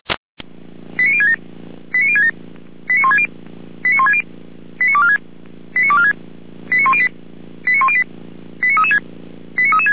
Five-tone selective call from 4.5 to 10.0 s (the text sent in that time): 89246 89246 89108 89108 89336 89336